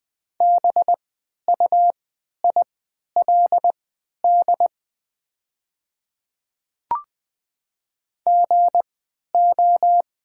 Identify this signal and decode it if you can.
Morse code, heard as a keyed tone, about 20 wpm, spaced as for 11 wpm, 700 Hz. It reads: BUILD GO
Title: 3x Character Spacing - Top 200 Words - 20wpm